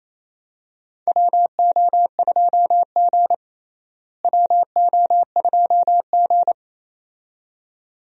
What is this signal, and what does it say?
Morse code, heard as a keyed tone, about 28 wpm, 700 Hz. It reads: WO2G WO2G
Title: USA Call Signs - 4 Characters - Repeated 2x - 28wpm